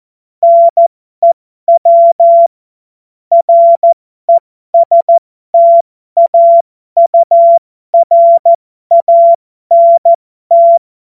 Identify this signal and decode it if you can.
Morse code, heard as a keyed tone, about 14 wpm, spaced as for 12 wpm, 680 Hz. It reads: NEW RESTAURANT
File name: Morsekurs / Witze